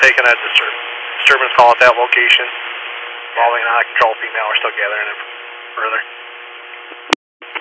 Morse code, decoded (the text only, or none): none